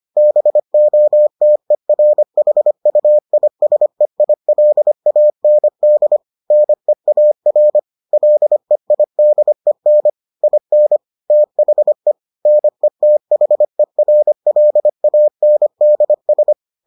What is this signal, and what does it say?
Morse code, heard as a keyed tone, about 25 wpm, 600 Hz. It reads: BOTERHUISEILAND NEAR LEIDEN IN THE NETHERLANDS